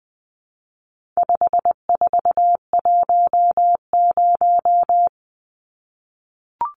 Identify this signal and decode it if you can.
Morse code, heard as a keyed tone, about 20 wpm, 700 Hz. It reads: 5410
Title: Numbers - 4-Digit 20wpm